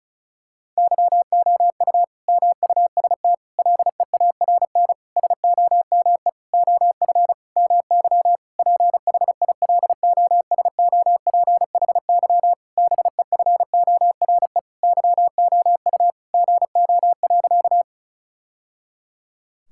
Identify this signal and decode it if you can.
Morse code, heard as a keyed tone, about 36 wpm, 700 Hz. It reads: YOU MUST LEARN SOME OF MY PHILOSOPHY BEFORE YOU GO.
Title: Sentences from Top 1000 Words - Review Part 6 - 35wpm